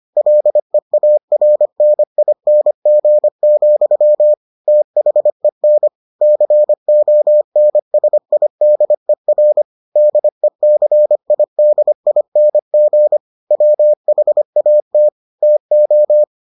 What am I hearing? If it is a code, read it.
Morse code, heard as a keyed tone, about 25 wpm, 600 Hz. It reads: LEARNING, THEN CONSIDER DECIDING WHAT TO